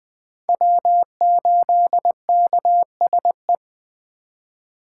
Morse code, heard as a keyed tone, about 24 wpm, 700 Hz. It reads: W8KSE
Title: Speed Racing - USA 1x3 Call Signs 20wpm